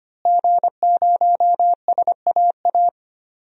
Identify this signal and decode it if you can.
Morse code, heard as a keyed tone, about 25 wpm, 700 Hz. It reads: G0SAA
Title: DXCC-100 Call Signs - 5 Characters - Repeated 2x - 25wpm